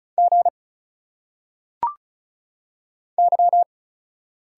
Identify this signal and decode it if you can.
Morse code, heard as a keyed tone, about 34 wpm, 700 Hz. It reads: G Y
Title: Single Letters - Phonetic Pronunciation 35wpm